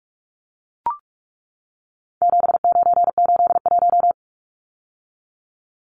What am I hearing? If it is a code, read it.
Morse code, heard as a keyed tone, about 44 wpm, 700 Hz. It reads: 7981
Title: Numbers - 4-Digit 45wpm